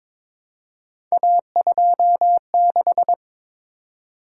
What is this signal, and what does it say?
Morse code, heard as a keyed tone, about 22 wpm, 700 Hz. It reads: A26
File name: practice / 22wpm / Three Letter-Number Combinations - 22wpm